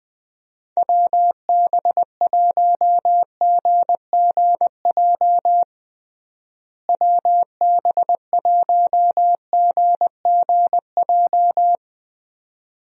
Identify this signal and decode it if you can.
Morse code, heard as a keyed tone, about 20 wpm, 700 Hz. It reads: WB1GGJ WB1GGJ
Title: USA Call Signs - 6 Characters - Repeated 2x - 20wpm